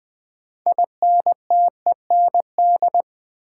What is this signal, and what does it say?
Morse code, heard as a keyed tone, about 20 wpm, 700 Hz. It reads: INTEND
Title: Sentences from Top 1600 Words - New Word Drill Alt - 20wpm